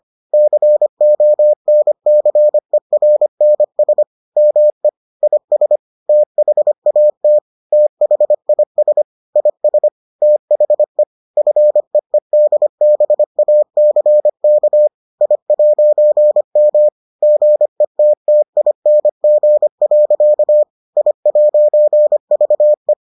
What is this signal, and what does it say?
Morse code, heard as a keyed tone, about 25 wpm, 600 Hz. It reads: CONCERNS ME IS THAT THIS IS THE FEEDBACK I'M GETTING. I'VE